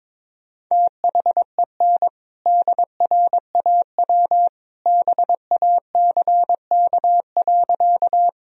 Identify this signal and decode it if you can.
Morse code, heard as a keyed tone, about 22 wpm, 700 Hz. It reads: THEN DRAW BACK.
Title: Sentences from Top 1000 Words - Review Part 1 - 22wpm